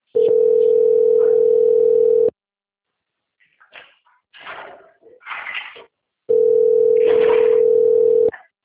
Ringback tone (North America), 440 + 480 Hz.